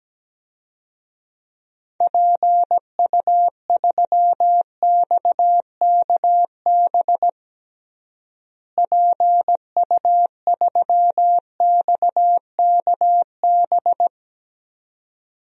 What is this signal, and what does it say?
Morse code, heard as a keyed tone, about 17 wpm, 700 Hz. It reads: PU3XKB PU3XKB
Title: DXCC-100 Call Signs - 6 Characters - Repeated 2x - 17wpm